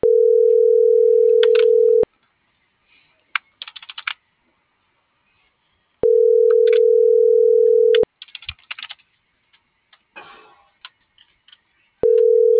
Ringback tone (North America), 440 + 480 Hz.
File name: CENTRAL MABE / out david.toribio